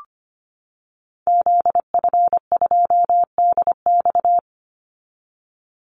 Morse code, heard as a keyed tone, about 25 wpm, 700 Hz. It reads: ZF2DX
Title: Call Signs - 25wpm